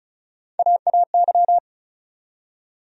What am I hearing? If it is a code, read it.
Morse code, heard as a keyed tone, about 36 wpm, 700 Hz. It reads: AAY
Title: USA Call Sign Postfixes - 3 Letters 35wpm